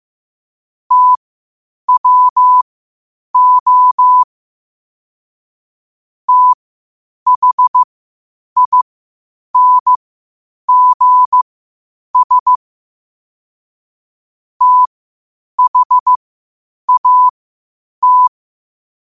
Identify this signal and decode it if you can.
Morse code, heard as a keyed tone, about 15 wpm, spaced as for 8 wpm, 1000 Hz. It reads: TWO THINGS THAT
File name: qotdfiles / 8wpm